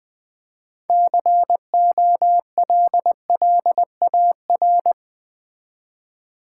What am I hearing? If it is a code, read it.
Morse code, heard as a keyed tone, about 20 wpm, 700 Hz. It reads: COLLAR
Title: Easy Words with Hints - 20wpm